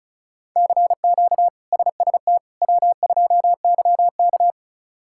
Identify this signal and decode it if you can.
Morse code, heard as a keyed tone, about 36 wpm, 700 Hz. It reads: CQ SST W2YK
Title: K1USN SST QSO Practice - 35wpm